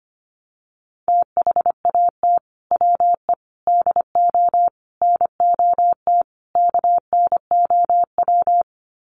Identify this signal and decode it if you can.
Morse code, heard as a keyed tone, about 25 wpm, 700 Hz. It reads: THAT WE DO NOT KNOW
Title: Sets of 5 Words Encore2 25wpm